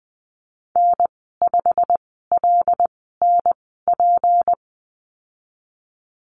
Morse code, heard as a keyed tone, about 20 wpm, spaced as for 15 wpm, 700 Hz. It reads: N5LNP
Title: 154 Lesson 25g - Letter p - Call Signs 20wpm